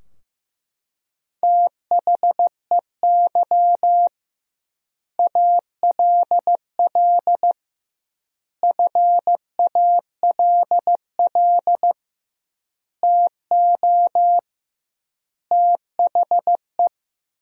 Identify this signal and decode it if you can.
Morse code, heard as a keyed tone, about 15 wpm, 700 Hz. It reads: THEY ALL FALL TO THE
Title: 2x Word-Spacing - Sentences from Top 500 Words - 15wpm